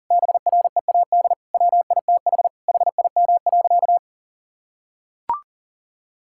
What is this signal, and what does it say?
Morse code, heard as a keyed tone, about 37 wpm, 700 Hz. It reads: BREAD WITH HIM.
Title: Sentences from Top 1600 Words 40wpm